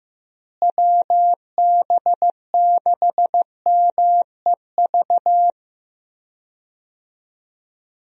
Morse code, heard as a keyed tone, about 16 wpm, 700 Hz. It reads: WB6MEV
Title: USA Call Signs - 6 Characters - Repeated 3x - 15wpm